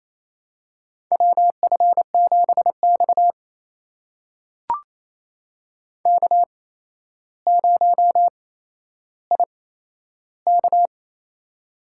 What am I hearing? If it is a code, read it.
Morse code, heard as a keyed tone, about 27 wpm, spaced as for 12 wpm, 700 Hz. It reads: WF7X K0IK
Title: USA Call Signs - 4 Characters - 8x Character Spacing - 28wpm